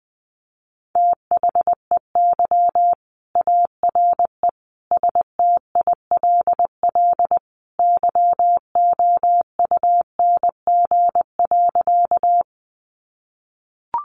Morse code, heard as a keyed tone, about 20 wpm, 700 Hz. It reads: THEY ARE STILL YOUNG.
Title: Sentences from Top 400 Words 20wpm